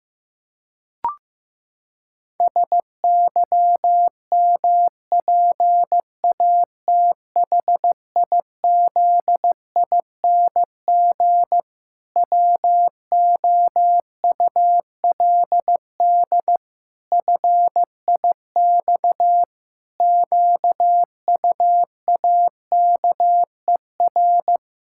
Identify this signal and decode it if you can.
Morse code, heard as a keyed tone, about 15 wpm, 700 Hz. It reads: SYMPATHIZING WOULD FIX QUAKER
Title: Pangram Challenge v2 - 15wpm